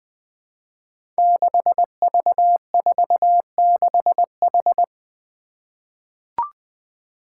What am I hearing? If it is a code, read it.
Morse code, heard as a keyed tone, about 20 wpm, 700 Hz. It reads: 6V46H
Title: Five Challenging Letter-Number Combinations 20wpm